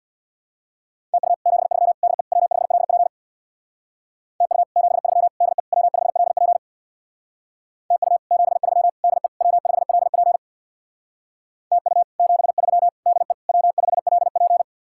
Morse code, heard as a keyed tone, about 67 wpm, 700 Hz. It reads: TU 73 DE W5LP TU 73 DE W5LP TU 73 DE W5LP TU 73 DE W5LP